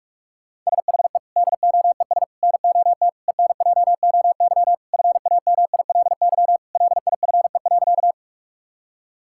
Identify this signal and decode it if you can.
Morse code, heard as a keyed tone, about 45 wpm, 700 Hz. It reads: SHE DOES NOT ENJOY FAMILY LIFE.
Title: Sentences from Top 1000 Words - Review Part 5 - 45wpm